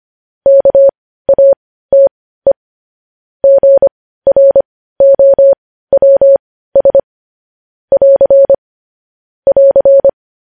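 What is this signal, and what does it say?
Morse code, heard as a keyed tone, about 25 wpm, spaced as for 15 wpm, 550 Hz. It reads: KATE GROWS <AR> <AR>
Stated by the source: Morse code news headlines at 15 wpm.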